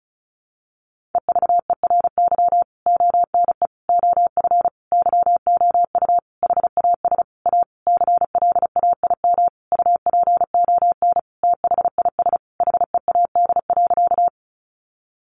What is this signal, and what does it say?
Morse code, heard as a keyed tone, about 34 wpm, 700 Hz. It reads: EVERY ONE OF YOU HAS A CLAIM UPON THIS HEAD.
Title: Sentences from Top 1500 Words 35wpm